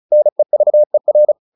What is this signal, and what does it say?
Morse code, heard as a keyed tone, about 35 wpm, 600 Hz. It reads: NEVER
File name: the old man and the sea 2 35wpm